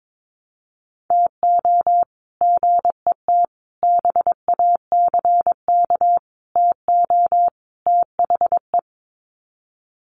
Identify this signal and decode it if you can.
Morse code, heard as a keyed tone, about 22 wpm, 700 Hz. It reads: TO GET BACK TO THE
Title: Sets of 5 Words Encore 22wpm